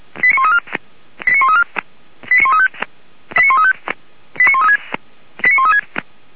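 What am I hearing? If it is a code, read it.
Five-tone selective call (ZVEI), heard as quick runs of five tones, tones of about 70 ms. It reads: 89134 89135 89135 89136 89136
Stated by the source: ZVEI five-tone call received by SDR, logged by Feuerwehr Orlamünde (fire brigade).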